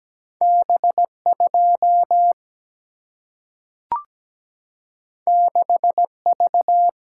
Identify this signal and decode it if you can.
Morse code, heard as a keyed tone, about 17 wpm, 700 Hz. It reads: B2 6V
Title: Single-Letter Single-Number 17wpm